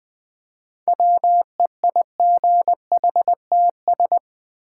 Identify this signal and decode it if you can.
Morse code, heard as a keyed tone, about 20 wpm, 700 Hz. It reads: WEIGHTS
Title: Easy Words with Hints - 20wpm